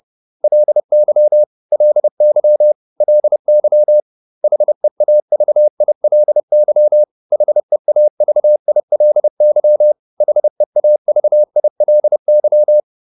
Morse code, heard as a keyed tone, about 30 wpm, 600 Hz. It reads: LY LY LY HEAVILY HEAVILY HEAVILY